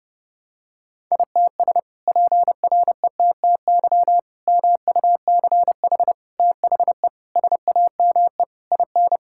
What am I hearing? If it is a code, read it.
Morse code, heard as a keyed tone, about 30 wpm, 700 Hz. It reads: ITS PRETTY MUCH THE SAME IN
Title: Sentences from Top 1000 Words - Review Part 6 - 30wpm